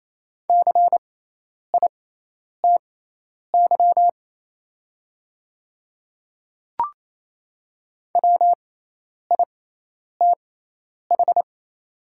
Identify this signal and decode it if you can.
Morse code, heard as a keyed tone, about 27 wpm, spaced as for 10 wpm, 700 Hz. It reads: CITY WITH
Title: 6x Character Spacing - Top 300 Words - 28wpm